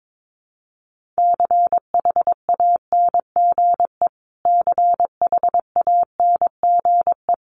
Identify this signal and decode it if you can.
Morse code, heard as a keyed tone, about 22 wpm, 700 Hz. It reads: CHANGE CHANGE
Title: Top 200 Words Repeated 22wpm